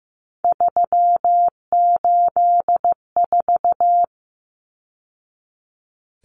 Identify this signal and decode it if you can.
Morse code, heard as a keyed tone, about 15 wpm, 700 Hz. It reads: 384
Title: Numbers - 3-Digit 15wpm